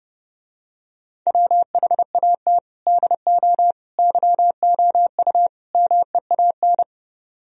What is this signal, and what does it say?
Morse code, heard as a keyed tone, about 30 wpm, 700 Hz. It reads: WHAT DO YOU MEAN